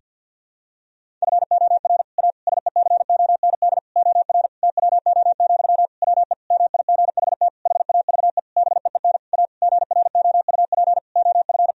Morse code, heard as a keyed tone, about 51 wpm, 700 Hz. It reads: FOR A SECOND OR TWO, WE MIGHT HAVE BEEN A GROUP OF
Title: Sentences from Top 1000 Words - Review Part 1 - 50wpm